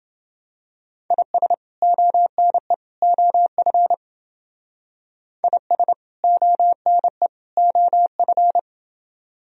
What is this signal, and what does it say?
Morse code, heard as a keyed tone, about 31 wpm, 700 Hz. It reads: IS ONE OF IS ONE OF